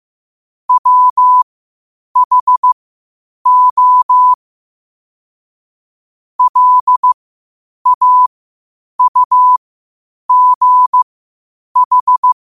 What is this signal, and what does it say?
Morse code, heard as a keyed tone, about 15 wpm, spaced as for 8 wpm, 1000 Hz. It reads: WHO LAUGH
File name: qotdfiles / 8wpm